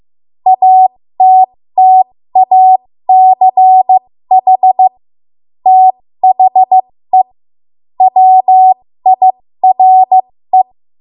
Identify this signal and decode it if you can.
Morse code, heard as a keyed tone, about 15 wpm, spaced as for 13 wpm, 750 Hz. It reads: ATTACH THE WIRE
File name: Morse / 13WPM